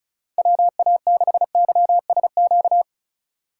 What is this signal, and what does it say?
Morse code, heard as a keyed tone, about 45 wpm, 700 Hz. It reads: WA6YSQ